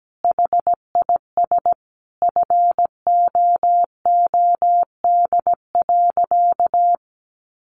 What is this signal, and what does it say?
Morse code, heard as a keyed tone, about 17 wpm, 700 Hz. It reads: HIS FOOD.
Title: Sentences from Top 1100 Words 17wpm